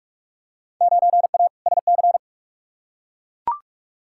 Morse code, heard as a keyed tone, about 41 wpm, 700 Hz. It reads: 9A SC
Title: Field Day Exchanges Encore - 45wpm